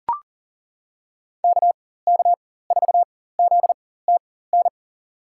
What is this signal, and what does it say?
Morse code, heard as a keyed tone, about 39 wpm, spaced as for 18 wpm, 700 Hz. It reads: KK4ZTN
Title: USA Call Signs - 6 Characters - 4x Character Spacing - 40wpm